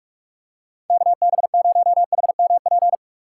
Morse code, heard as a keyed tone, about 45 wpm, 700 Hz. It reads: KB0HMP
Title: USA Call Signs - 3x Character Spacing - 45wpm